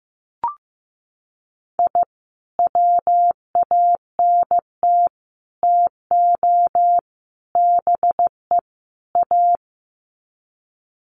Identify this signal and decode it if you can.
Morse code, heard as a keyed tone, about 15 wpm, 700 Hz. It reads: I WANT TO BE A
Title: Sets of 5 Words Encore 15wpm